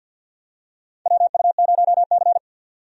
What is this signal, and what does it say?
Morse code, heard as a keyed tone, about 47 wpm, 700 Hz. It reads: WU0C
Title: USA Call Signs - 4 Characters - 50wpm